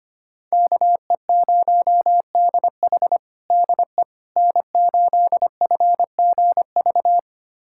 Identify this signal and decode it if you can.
Morse code, heard as a keyed tone, about 25 wpm, 700 Hz. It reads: KE0DH DE N8FGV
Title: USA Callsign DE Callsign - 5 Characters - 25wpm